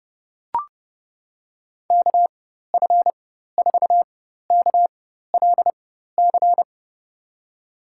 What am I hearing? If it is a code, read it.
Morse code, heard as a keyed tone, about 30 wpm, spaced as for 14 wpm, 700 Hz. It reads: KF4KLC
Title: USA Call Signs - 6 Characters - 4x Character Spacing - 30wpm